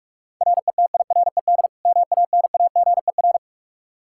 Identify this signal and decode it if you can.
Morse code, heard as a keyed tone, about 46 wpm, 700 Hz. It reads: RETIRED MANAGER
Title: K1USN SST QSO Practice 45wpm